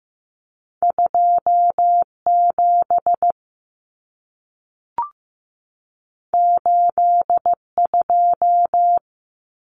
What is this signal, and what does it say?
Morse code, heard as a keyed tone, about 15 wpm, 700 Hz. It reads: 27 82
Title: Numbers - 2-Digit 15wpm